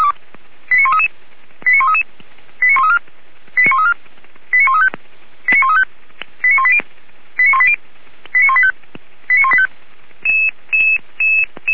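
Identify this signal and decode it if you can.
Five-tone selective call (ZVEI), heard as quick runs of five tones, tones of about 70 ms. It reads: 89133 89133 89134 89134 89135 89135 89180 89180 89185 89185
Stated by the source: ZVEI five-tone call received by SDR.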